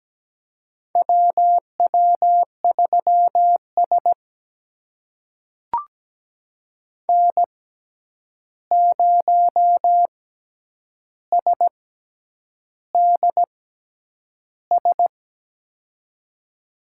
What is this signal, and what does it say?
Morse code, heard as a keyed tone, about 17 wpm, spaced as for 8 wpm, 700 Hz. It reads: WW3S N0SDS
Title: USA Call Signs - 6x Character Spacing - 17wpm